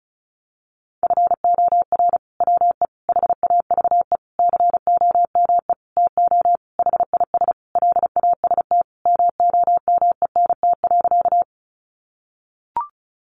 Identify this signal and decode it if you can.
Morse code, heard as a keyed tone, about 34 wpm, 700 Hz. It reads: FOR WE HAVE COME TO HIS LAST MOMENT.
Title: Sentences from Top 1000 Words 35wpm